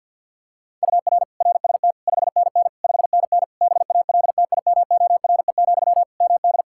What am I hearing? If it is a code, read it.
Morse code, heard as a keyed tone, about 53 wpm, 700 Hz. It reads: UR RST 5NN 5NN BALTIMORE, MD
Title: Practice QSO - Exchange 1 - Beginning - 50wpm